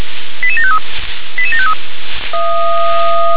Five-tone selective call (ZVEI), heard as quick runs of five tones, tones of about 70 ms. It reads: 88753 88753
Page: SDR/ZVEI